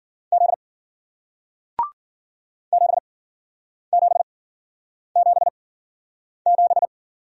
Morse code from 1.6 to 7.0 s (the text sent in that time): 7777